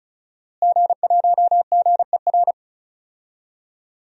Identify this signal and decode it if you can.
Morse code, heard as a keyed tone, about 36 wpm, 700 Hz. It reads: G1GER
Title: DXCC-100 Call Signs - 5 Characters - Repeated 3x - 35wpm